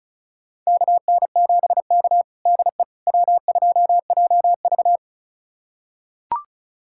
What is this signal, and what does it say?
Morse code, heard as a keyed tone, about 36 wpm, 690 Hz. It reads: KN7K DE W2JV
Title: USA Callsign DE Callsign - 4 Characters - 35wpm